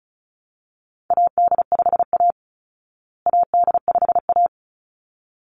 Morse code, heard as a keyed tone, about 34 wpm, 700 Hz. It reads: AD5A AD5A
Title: Call Signs Repeated 3x - 35wpm